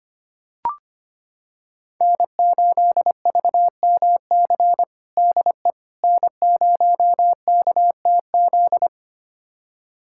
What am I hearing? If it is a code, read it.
Morse code, heard as a keyed tone, about 25 wpm, 700 Hz. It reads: N8VMC DE N0KTZ
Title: USA Callsign DE Callsign - 5 Characters - 25wpm